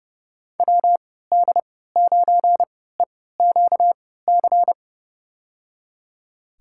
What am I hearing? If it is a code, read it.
Morse code, heard as a keyed tone, about 29 wpm, spaced as for 17 wpm, 700 Hz. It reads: WD9EQC
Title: USA Call Signs - 6 Characters - 3x Character Spacing - 30wpm